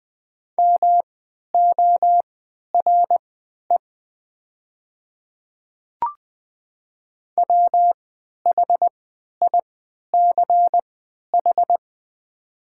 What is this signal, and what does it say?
Morse code, heard as a keyed tone, about 20 wpm, spaced as for 11 wpm, 700 Hz. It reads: MORE WHICH
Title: 3x Character Spacing - Top 200 Words - 20wpm